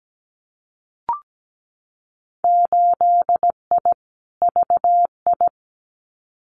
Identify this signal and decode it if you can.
Morse code, heard as a keyed tone, about 17 wpm, 700 Hz. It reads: 8I VI
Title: Winter Field Day Exchanges Encore 17wpm